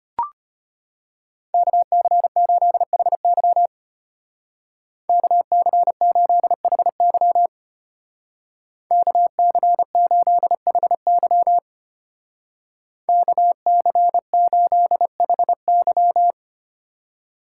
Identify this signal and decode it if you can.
Morse code, heard as a keyed tone, about 31 wpm, 700 Hz. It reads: KC8HY KC8HY KC8HY KC8HY
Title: Speed Racing - USA 2x2 Call Signs 25wpm